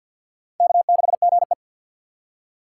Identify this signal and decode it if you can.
Morse code, heard as a keyed tone, about 52 wpm, 690 Hz. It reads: K6GE
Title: USA Call Signs - 2x Character Spacing - 50wpm